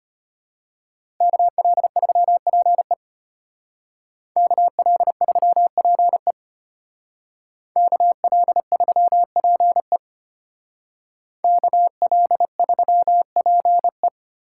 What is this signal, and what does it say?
Morse code, heard as a keyed tone, about 32 wpm, 700 Hz. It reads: KL3PE KL3PE KL3PE KL3PE